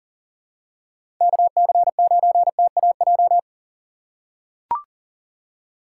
Morse code, heard as a keyed tone, about 39 wpm, 700 Hz. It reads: KC9TAJ
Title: USA Call Signs - 3x Character Spacing - 40wpm